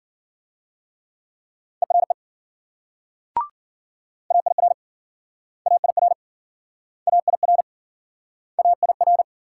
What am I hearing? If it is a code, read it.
Morse code, heard as a keyed tone, about 56 wpm, 700 Hz. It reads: ERE AIR AIR AIR AIR